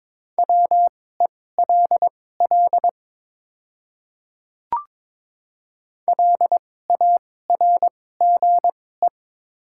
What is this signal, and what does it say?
Morse code, heard as a keyed tone, about 22 wpm, spaced as for 16 wpm, 700 Hz. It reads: WELL LARGE
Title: 2x Character Spacing - Top 200 Words - 22wpm